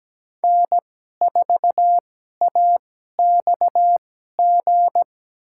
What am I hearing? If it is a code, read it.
Morse code, heard as a keyed tone, about 17 wpm, spaced as for 12 wpm, 700 Hz. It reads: N4AXG
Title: USA Call Signs - 5 Characters - 2x Character Spacing - 17wpm